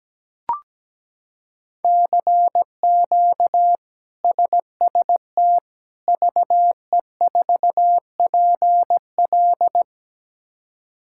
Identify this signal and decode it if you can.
Morse code, heard as a keyed tone, about 17 wpm, 700 Hz. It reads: CQ SST VE4PL
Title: K1USN SST QSO Practice - 17wpm